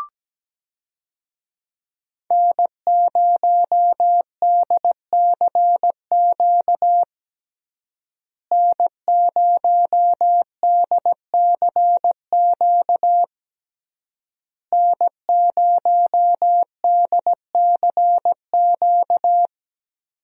Morse code, heard as a keyed tone, about 17 wpm, 700 Hz. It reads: N0DCQ N0DCQ N0DCQ
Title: USA Call Signs - 5 Characters - Repeated 3x - 17wpm